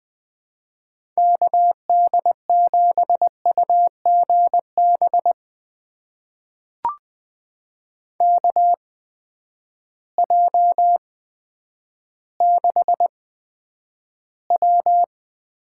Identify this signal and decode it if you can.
Morse code, heard as a keyed tone, about 20 wpm, spaced as for 10 wpm, 700 Hz. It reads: KD7UGB KJ6W
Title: USA Call Signs - 6 Characters - 8x Character Spacing - 20wpm